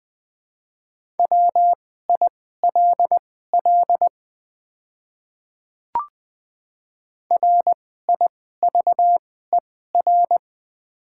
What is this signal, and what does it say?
Morse code, heard as a keyed tone, about 20 wpm, spaced as for 14 wpm, 700 Hz. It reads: WILL RIVER
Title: 2x Character Spacing - Top 300 Words - 20wpm